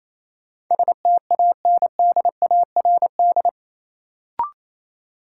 Morse code, heard as a keyed tone, about 28 wpm, 700 Hz. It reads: STANDARD
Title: Sentences from Top 1000 Words - Review Part 6 - New Word Drill - 28wpm